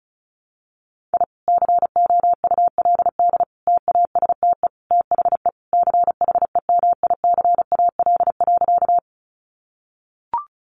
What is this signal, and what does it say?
Morse code, heard as a keyed tone, about 34 wpm, 700 Hz. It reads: I COULD TASTE THE CHEMICAL.
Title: Sentences from Top 1200 Words 35wpm